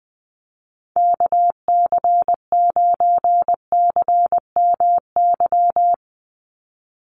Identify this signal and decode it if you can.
Morse code, heard as a keyed tone, about 20 wpm, 700 Hz. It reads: KC9CMY
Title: USA Call Signs 20wpm